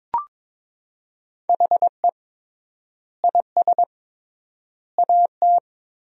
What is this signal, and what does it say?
Morse code, heard as a keyed tone, about 22 wpm, 700 Hz. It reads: HE IS AT
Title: 3x Word-Spacing - Sentences from Top 300 Words - 22wpm